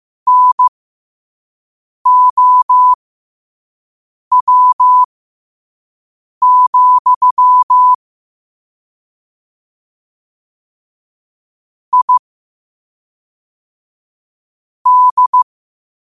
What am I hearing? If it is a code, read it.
Morse code, heard as a keyed tone, about 15 wpm, spaced as for 5 wpm, 1000 Hz. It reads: NOW, I D